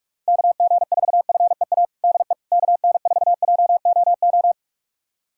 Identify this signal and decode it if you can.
Morse code, heard as a keyed tone, about 46 wpm, 700 Hz. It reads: KG4FEA DE KN4JOO
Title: USA Callsign DE Callsign - 6 Characters - 45wpm